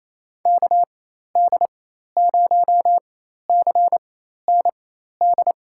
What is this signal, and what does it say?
Morse code, heard as a keyed tone, about 28 wpm, spaced as for 13 wpm, 700 Hz. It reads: KD0CND
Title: USA Call Signs - 4x Character Spacing - 28wpm